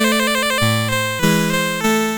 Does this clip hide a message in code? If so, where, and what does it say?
No.